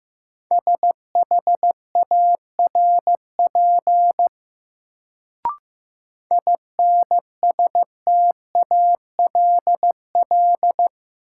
Morse code, heard as a keyed tone, about 15 wpm, 700 Hz. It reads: SHARP INSTALL
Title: Sentences from Top 1600 Words - New Word Drill - 15wpm